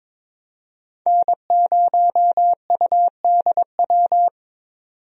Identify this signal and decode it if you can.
Morse code, heard as a keyed tone, about 22 wpm, 700 Hz. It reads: N0UDW
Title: USA Call Signs - 5 Characters - 22wpm